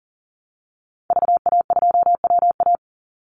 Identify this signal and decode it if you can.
Morse code, heard as a keyed tone, about 39 wpm, 700 Hz. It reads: VA2WA
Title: Call Signs Repeated 3x - 40wpm